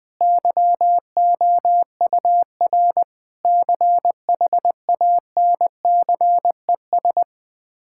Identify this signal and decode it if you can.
Morse code, heard as a keyed tone, about 20 wpm, 700 Hz. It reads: YOUR CHANCES
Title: Sets of 8 Words - 20wpm